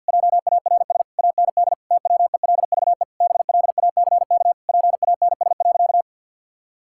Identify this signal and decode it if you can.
Morse code, heard as a keyed tone, about 52 wpm, 700 Hz. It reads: JARS AND TWELVE BLACK PANS.
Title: Pangram Challenge v2 - 50wpm